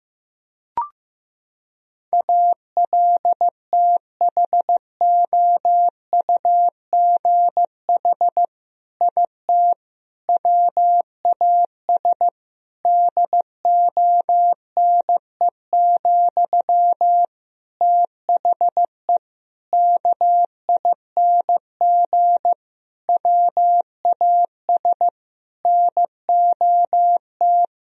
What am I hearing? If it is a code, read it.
Morse code, heard as a keyed tone, about 15 wpm, 700 Hz. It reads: ALTHOUGH IT WAS DONE, THE KING WAS NOT
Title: Sentences from Top 1000 Words - Review Part 3 - 15wpm